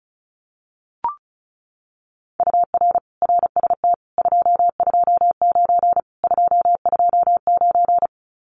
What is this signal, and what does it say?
Morse code, heard as a keyed tone, about 34 wpm, 700 Hz. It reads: UR RST 229 229
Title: Common QSO Phrases Encore 35wpm